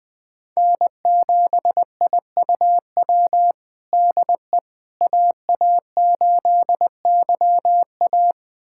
Morse code, heard as a keyed tone, about 20 wpm, 700 Hz. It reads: N7IUW DE AA8YA
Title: USA Callsign DE Callsign - 5 Characters - 20wpm